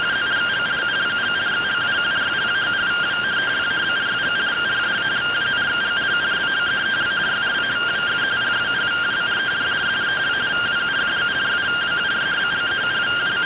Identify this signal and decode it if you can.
Two-tone teleprinter signal, about 170 Hz shift.
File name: findings / CT AXIAL -256.gen-39 rtty rx